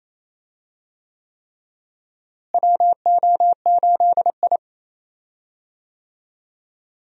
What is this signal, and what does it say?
Morse code, heard as a keyed tone, about 28 wpm, 700 Hz. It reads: WO8I
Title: USA Call Signs - 4 Characters - Repeated 3x - 28wpm